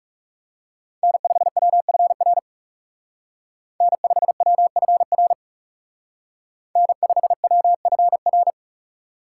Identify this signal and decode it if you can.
Morse code, heard as a keyed tone, about 35 wpm, 700 Hz. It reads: N5WFR N5WFR N5WFR
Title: Speed Racing - USA Call Signs 30wpm